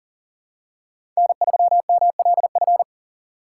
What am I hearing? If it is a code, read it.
Morse code, heard as a keyed tone, about 40 wpm, 700 Hz. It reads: N3MLF
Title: USA Call Signs - 6x Character Spacing - 40wpm